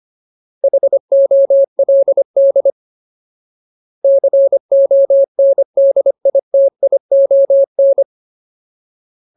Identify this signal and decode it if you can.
Morse code, heard as a keyed tone, about 25 wpm, 550 Hz. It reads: HOLD CONDITION
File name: Ultimate Word List / 25WPM Ultimate